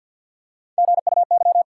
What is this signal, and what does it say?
Morse code, heard as a keyed tone, about 52 wpm, 700 Hz. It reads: GUY
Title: Three Letter Words - 50wpm